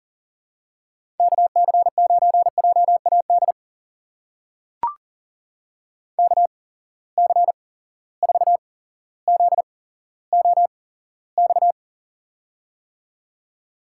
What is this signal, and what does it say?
Morse code, heard as a keyed tone, about 38 wpm, spaced as for 17 wpm, 700 Hz. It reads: KC9JAD KC4ZOX
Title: USA Call Signs - 6 Characters - 8x Character Spacing - 40wpm